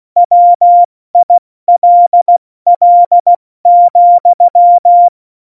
Morse code, heard as a keyed tone, about 16 wpm, spaced as for 14 wpm, 700 Hz. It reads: WILL,
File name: Morsekurs / Witze / Witz 21. Januar 2020, 70 F80